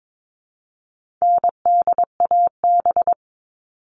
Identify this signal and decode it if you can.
Morse code, heard as a keyed tone, about 22 wpm, 700 Hz. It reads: NDAB
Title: All Cut Numbers - 4-Digit 22wpm